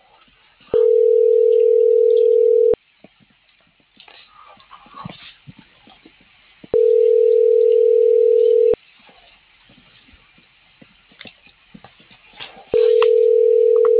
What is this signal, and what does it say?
Ringback tone (North America), 440 + 480 Hz.